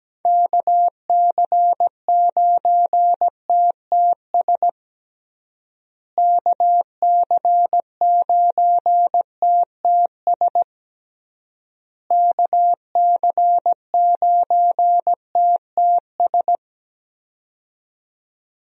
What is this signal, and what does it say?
Morse code, heard as a keyed tone, about 17 wpm, 700 Hz. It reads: KC9TTS KC9TTS KC9TTS
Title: USA Call Signs - 6 Characters - Repeated 3x - 17wpm